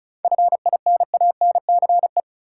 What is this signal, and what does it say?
Morse code, heard as a keyed tone, about 34 wpm, 700 Hz. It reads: FINANCE
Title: Seven Letter Words - 35wpm